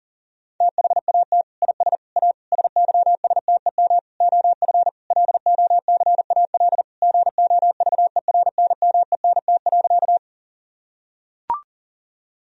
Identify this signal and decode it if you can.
Morse code, heard as a keyed tone, about 40 wpm, 700 Hz. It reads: THAT IS A SYSTEM OF LOCAL GOVERNMENT.
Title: Sentences from Top 1000 Words - Review Part 3 - 40wpm